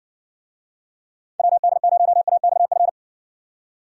Morse code, heard as a keyed tone, about 60 wpm, 700 Hz. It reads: WD0AXF